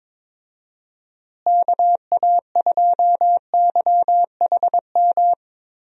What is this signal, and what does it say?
Morse code, heard as a keyed tone, about 22 wpm, 700 Hz. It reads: KA2YHM
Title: USA Call Signs - 6 Characters - Repeated 3x - 22wpm